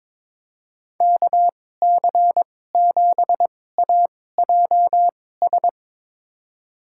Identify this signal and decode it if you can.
Morse code, heard as a keyed tone, about 22 wpm, spaced as for 16 wpm, 700 Hz. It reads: KC7AJS
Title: USA Call Signs - 6 Characters - 2x Character Spacing - 22wpm